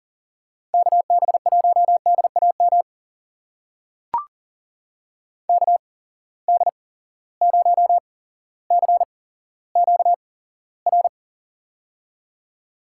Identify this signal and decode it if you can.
Morse code, heard as a keyed tone, about 40 wpm, spaced as for 17 wpm, 700 Hz. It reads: KB1DAM KD0CQR
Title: USA Call Signs - 6 Characters - 8x Character Spacing - 40wpm